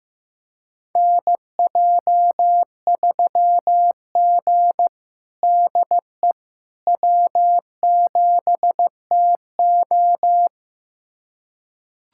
Morse code, heard as a keyed tone, about 15 wpm, 700 Hz. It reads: NJ3G DE W7TO
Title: USA Callsign DE Callsign - 4 Characters - 15wpm